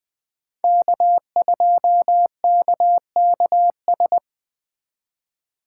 Morse code, heard as a keyed tone, about 20 wpm, 700 Hz. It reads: K2KKS